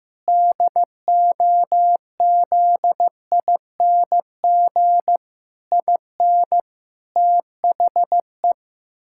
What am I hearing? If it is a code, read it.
Morse code, heard as a keyed tone, about 15 wpm, 700 Hz. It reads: DOZING IN THE
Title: Pangram Challenge v2 - 15wpm